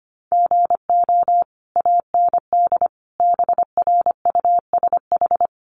Morse code, heard as a keyed tone, about 25 wpm, 700 Hz. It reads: GO AND BRUSH
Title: Sentences from Top 1100 Words 25wpm